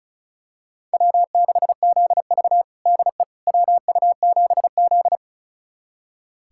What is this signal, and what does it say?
Morse code, heard as a keyed tone, about 36 wpm, 700 Hz. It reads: W6ZV DE WU7Z
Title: USA Callsign DE Callsign - 4 Characters - 35wpm